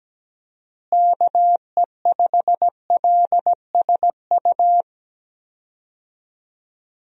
Morse code, heard as a keyed tone, about 17 wpm, 700 Hz. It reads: KE5LSU
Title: USA Call Signs - 6 Characters - Repeated 2x - 17wpm